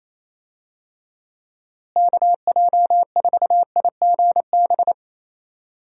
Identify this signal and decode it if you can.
Morse code, heard as a keyed tone, about 28 wpm, 700 Hz. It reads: KJ4IGB
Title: USA Call Signs - 6 Characters - Repeated 3x - 28wpm